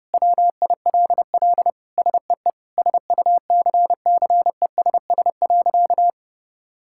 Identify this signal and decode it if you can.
Morse code, heard as a keyed tone, about 32 wpm, 700 Hz. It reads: WILL SEE SUCCESS.
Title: Sentences from Top 1000 Words - Review Part 5 - 30wpm